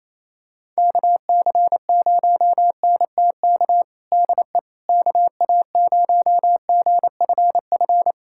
Morse code, heard as a keyed tone, about 28 wpm, 700 Hz. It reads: KC0NTK DE KA0GFF
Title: USA Callsign DE Callsign 28wpm